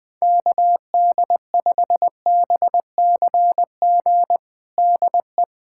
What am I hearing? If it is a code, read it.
Morse code, heard as a keyed tone, about 21 wpm, 700 Hz. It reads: KD5BCG DE
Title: USA Callsign DE Callsign 20wpm